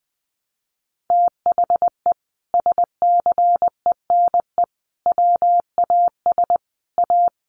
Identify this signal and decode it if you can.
Morse code, heard as a keyed tone, about 20 wpm, 700 Hz. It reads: THE SCENE WAS A
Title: Sentences from Top 1100 Words 20wpm